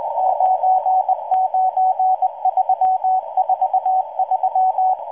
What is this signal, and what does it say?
Morse code, heard as a keyed tone, about 19 wpm, 750 Hz, with radio noise. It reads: 99343